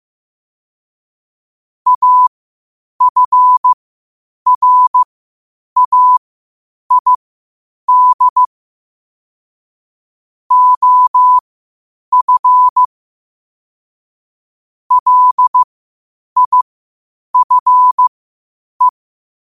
Morse code, heard as a keyed tone, about 15 wpm, spaced as for 8 wpm, 1000 Hz. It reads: AFRAID OF LIFE
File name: qotdfiles / 8wpm